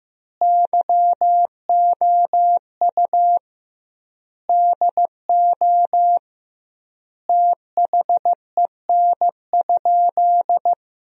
Morse code, heard as a keyed tone, about 15 wpm, 700 Hz. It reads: YOU DO THEN?
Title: 2x Word-Spacing - Sentences from Top 100 Words - 15wpm